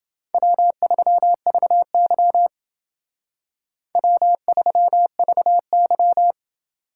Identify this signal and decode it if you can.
Morse code, heard as a keyed tone, about 28 wpm, 700 Hz. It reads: W3VY W3VY